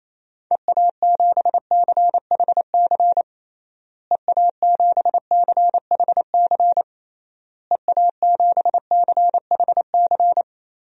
Morse code, heard as a keyed tone, about 28 wpm, 700 Hz. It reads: EA7CHC EA7CHC EA7CHC
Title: DXCC-100 Call Signs - 6 Characters - Repeated 3x - 28wpm